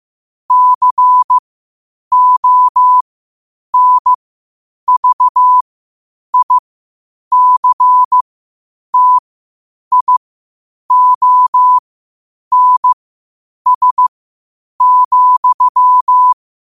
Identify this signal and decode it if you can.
Morse code, heard as a keyed tone, about 15 wpm, spaced as for 8 wpm, 1000 Hz. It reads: CONVICTIONS,
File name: qotdfiles / 8wpm